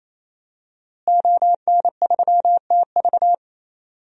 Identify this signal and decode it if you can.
Morse code, heard as a keyed tone, about 28 wpm, 700 Hz. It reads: ON3TV
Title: DXCC-100 Call Signs - 5 Characters - Repeated 3x - 28wpm